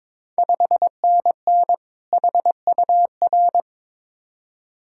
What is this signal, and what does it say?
Morse code, heard as a keyed tone, about 22 wpm, 690 Hz. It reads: 5NN HUR
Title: Ontario QSO Party Exchange 22wpm